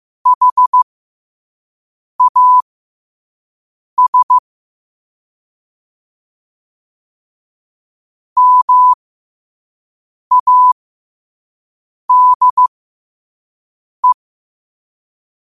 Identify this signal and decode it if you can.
Morse code, heard as a keyed tone, about 15 wpm, spaced as for 5 wpm, 1000 Hz. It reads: HAS MADE
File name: qotdfiles / 5wpm